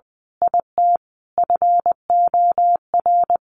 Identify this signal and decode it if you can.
Morse code, heard as a keyed tone, about 20 wpm, 700 Hz. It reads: IT FOR
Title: Sentences from Top 1400 Words 20wpm